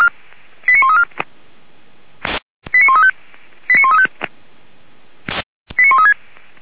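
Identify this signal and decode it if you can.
Five-tone selective call (ZVEI), heard as quick runs of five tones, tones of about 80 ms. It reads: 89134 89135 89135 89136
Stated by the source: ZVEI five-tone selective call received by SDR, fire brigade Orlamünde (Feuerwehr Orlamünde) station f85360.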